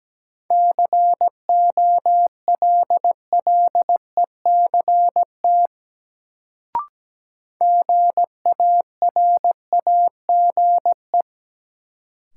Morse code, heard as a keyed tone, about 17 wpm, 700 Hz. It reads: COLLECT GARAGE
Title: Sentences from Top 1600 Words - New Word Drill Alt - 17wpm